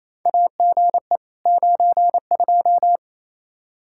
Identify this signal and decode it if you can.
Morse code, heard as a keyed tone, about 28 wpm, 700 Hz. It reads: AGE 92
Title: Practice QSO - Exchange 3 - More Info - 28wpm